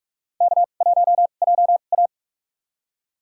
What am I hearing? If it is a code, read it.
Morse code, heard as a keyed tone, about 46 wpm, spaced as for 32 wpm, 700 Hz. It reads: K1JA
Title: USA Call Signs - 2x Character Spacing - 45wpm